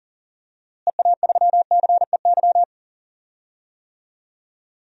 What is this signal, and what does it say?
Morse code, heard as a keyed tone, about 40 wpm, 700 Hz. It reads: EA3CEY
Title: DXCC-100 Call Signs - 6 Characters - Repeated 2x - 40wpm